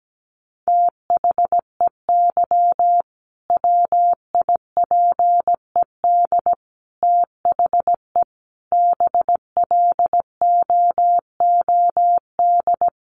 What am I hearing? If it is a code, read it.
Morse code, heard as a keyed tone, about 17 wpm, 700 Hz. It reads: THEY WIPED THE BLOOD
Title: A Sound Of Thunder 17wpm